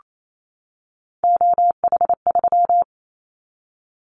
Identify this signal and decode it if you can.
Morse code, heard as a keyed tone, about 28 wpm, 700 Hz. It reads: OH3
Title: Callsign Prefixes for Top 50 DXCC - 28wpm